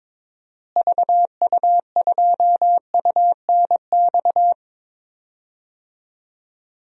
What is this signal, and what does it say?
Morse code, heard as a keyed tone, about 22 wpm, 700 Hz. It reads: VU2UNX
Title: DXCC-100 Call Signs - 6 Characters - Repeated 3x - 22wpm